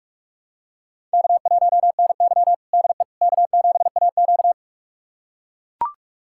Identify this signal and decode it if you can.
Morse code, heard as a keyed tone, about 41 wpm, 700 Hz. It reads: K1NY DE K7AQ
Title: USA Callsign DE Callsign - 4 Characters - 45wpm